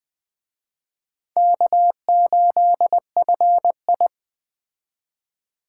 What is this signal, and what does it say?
Morse code, heard as a keyed tone, about 20 wpm, 700 Hz. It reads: K8FI
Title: Speed Racing - USA 1x2 Call Signs 17wpm